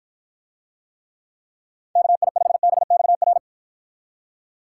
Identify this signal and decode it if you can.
Morse code, heard as a keyed tone, about 54 wpm, 700 Hz. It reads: KI5BXR